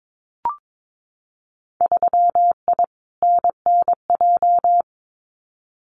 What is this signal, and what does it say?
Morse code, heard as a keyed tone, about 22 wpm, 700 Hz. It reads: 3I NNJ
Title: Winter Field Day Exchanges 22wpm